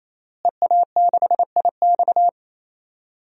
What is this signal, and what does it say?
Morse code, heard as a keyed tone, about 28 wpm, 700 Hz. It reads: EA6IX
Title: DXCC-100 Call Signs - 5 Characters - Repeated 3x - 28wpm